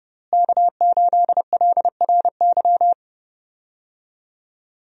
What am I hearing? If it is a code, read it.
Morse code, heard as a keyed tone, about 30 wpm, 700 Hz. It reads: K8LRY
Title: USA Call Signs - 5 Characters - Repeated 3x - 30wpm